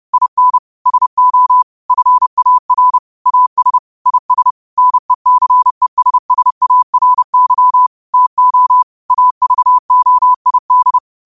Morse code, heard as a keyed tone, about 30 wpm, 1000 Hz. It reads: IN SO FAR AS IS NECESSARY TO AVOID
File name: qotdfiles / 30wpm